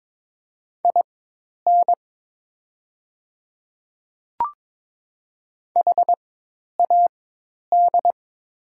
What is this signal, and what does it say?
Morse code, heard as a keyed tone, about 22 wpm, spaced as for 10 wpm, 700 Hz. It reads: IN HAD
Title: 4x Character Spacing - Top 200 Words - 22wpm